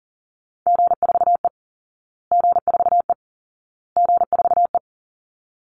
Morse code, heard as a keyed tone, about 40 wpm, 700 Hz. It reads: G4E G4E G4E
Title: Call Signs Repeated 2x - 40wpm